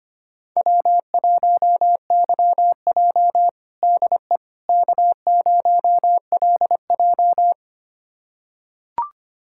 Morse code, heard as a keyed tone, about 25 wpm, 700 Hz. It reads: W1YJ DE K0LJ
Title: USA Callsign DE Callsign - 4 Characters - 25wpm